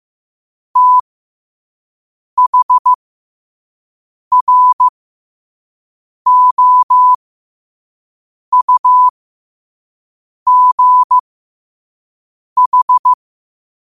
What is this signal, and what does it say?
Morse code, heard as a keyed tone, about 15 wpm, spaced as for 5 wpm, 1000 Hz. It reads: THROUGH
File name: qotdfiles / 5wpm